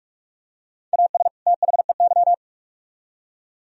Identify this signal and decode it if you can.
Morse code, heard as a keyed tone, about 44 wpm, 700 Hz. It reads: AS THEY